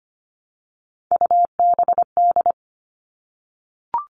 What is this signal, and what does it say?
Morse code, heard as a keyed tone, about 25 wpm, 700 Hz. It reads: UBD
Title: All Cut Numbers - 3-Digit 25wpm